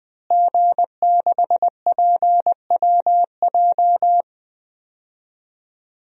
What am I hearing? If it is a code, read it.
Morse code, heard as a keyed tone, about 20 wpm, 700 Hz. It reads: G6PWJ
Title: DXCC-100 Call Signs - 5 Characters - Repeated 3x - 20wpm